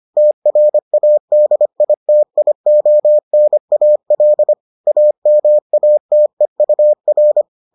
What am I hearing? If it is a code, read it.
Morse code, heard as a keyed tone, about 25 wpm, 600 Hz. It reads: TRADITIONAL AMATEUR